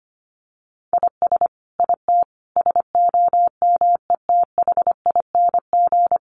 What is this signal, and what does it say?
Morse code, heard as a keyed tone, about 25 wpm, 700 Hz. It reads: IS IT SOMETHING
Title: Sentences from Top 700 Words 25wpm